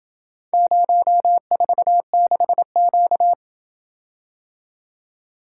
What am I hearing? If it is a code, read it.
Morse code, heard as a keyed tone, about 27 wpm, 700 Hz. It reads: 046Q